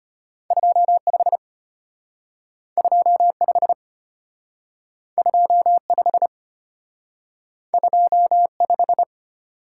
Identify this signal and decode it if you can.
Morse code, heard as a keyed tone, about 33 wpm, 700 Hz. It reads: 25 25 25 25